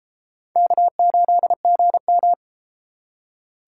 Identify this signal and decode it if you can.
Morse code, heard as a keyed tone, about 31 wpm, 700 Hz. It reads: K8GM